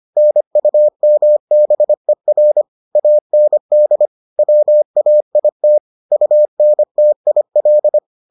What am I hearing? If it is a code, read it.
Morse code, heard as a keyed tone, about 25 wpm, 600 Hz. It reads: NUMBER AND WAIT UNTIL